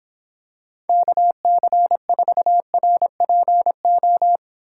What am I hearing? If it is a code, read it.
Morse code, heard as a keyed tone, about 26 wpm, 700 Hz. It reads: KC4RPO